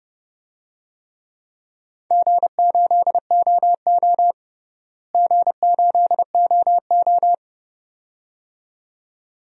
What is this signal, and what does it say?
Morse code, heard as a keyed tone, about 30 wpm, 700 Hz. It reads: G8OO G8OO
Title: DXCC-100 Call Signs - 4 Characters - Repeated 2x - 30wpm